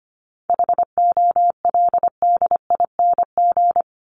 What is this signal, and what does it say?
Morse code, heard as a keyed tone, about 25 wpm, 700 Hz. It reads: HOLDING
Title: ING Suffix - Top 100 Words with Base Words 25wpm